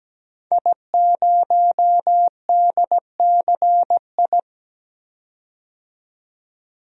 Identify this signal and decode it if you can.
Morse code, heard as a keyed tone, about 17 wpm, 700 Hz. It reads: I0DCI
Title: DXCC-100 Call Signs - 5 Characters - 17wpm